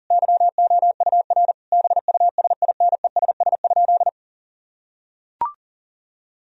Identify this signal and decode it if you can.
Morse code, heard as a keyed tone, about 40 wpm, 700 Hz. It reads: YOUR BUSINESS?
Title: Sentences from Top 1000 Words - Review Part 4 - 40wpm